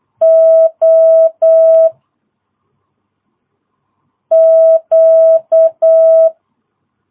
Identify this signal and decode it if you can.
Morse code, heard as a keyed tone, about 8 wpm, 660 Hz. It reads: O Q
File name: jh8cbh / 2022cw